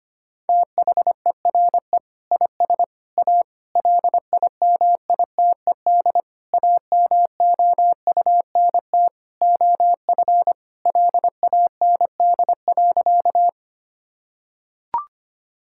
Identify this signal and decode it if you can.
Morse code, heard as a keyed tone, about 25 wpm, 700 Hz. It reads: THERE IS A LIMITED AMOUNT OF LAND.
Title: Sentences from Top 1000 Words - Review Part 6 - 25wpm